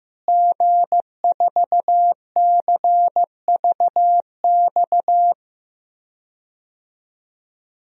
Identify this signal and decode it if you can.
Morse code, heard as a keyed tone, about 15 wpm, 700 Hz. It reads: G4CVX
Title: DXCC-100 Call Signs - 5 Characters - Repeated 2x - 15wpm